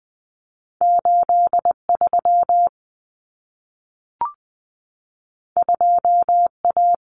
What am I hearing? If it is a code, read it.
Morse code, heard as a keyed tone, about 20 wpm, 700 Hz. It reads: 83 2A
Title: Common Cut Numbers - 2-Digit 20wpm